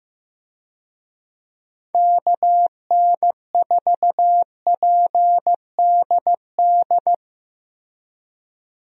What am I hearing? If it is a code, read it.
Morse code, heard as a keyed tone, about 15 wpm, 700 Hz. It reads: KN4PDD
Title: USA Call Signs - 6 Characters - Repeated 2x - 15wpm